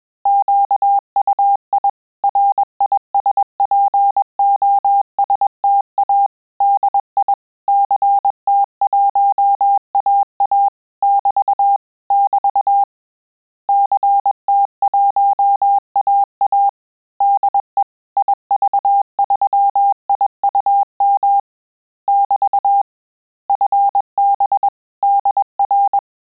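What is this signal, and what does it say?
Morse code, heard as a keyed tone, about 21 wpm, 800 Hz. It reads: QUI RISPOSTA DI CT1AA <BT> <BT> CT1AA DE IV3IUM <BT> FB DR
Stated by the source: Morse QSO practice at 20 words per minute, in Italian and English.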